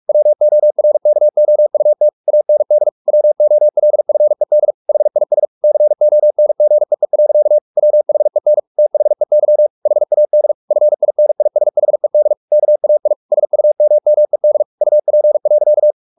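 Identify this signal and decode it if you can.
Morse code, heard as a keyed tone, about 44 wpm, 600 Hz. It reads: WORKOUT AND WOLFED HIS CONGEE. WHEN THEY HAD FINISHED KAI SUMMED UP.
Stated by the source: Morse practice at 45 wpm, the novel Never by Ken Follett in English, from CW med Gnister.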